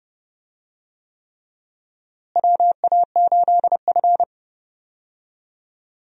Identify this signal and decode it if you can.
Morse code, heard as a keyed tone, about 30 wpm, 700 Hz. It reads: WA8F
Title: USA Call Signs - 4 Characters - Repeated 3x - 30wpm